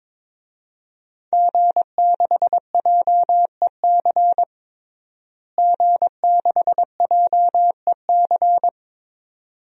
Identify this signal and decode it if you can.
Morse code, heard as a keyed tone, about 22 wpm, 700 Hz. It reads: G6JEC G6JEC
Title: DXCC-100 Call Signs - 5 Characters - Repeated 2x - 22wpm